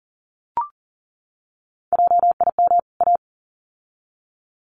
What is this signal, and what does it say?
Morse code, heard as a keyed tone, about 40 wpm, 700 Hz. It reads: JIM A
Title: CWT Exchange 40wpm